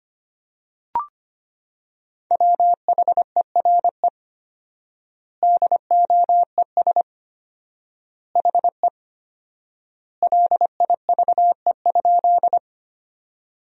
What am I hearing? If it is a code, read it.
Morse code, heard as a keyed tone, about 25 wpm, 700 Hz. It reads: WHERE DOES HE LIVE?
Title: 4x Word-Spacing - Sentences from Top 200 Words - 25wpm